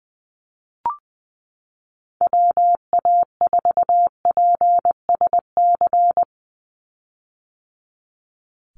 Morse code, heard as a keyed tone, about 20 wpm, 700 Hz. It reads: WA4PSC
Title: Call Signs - 20wpm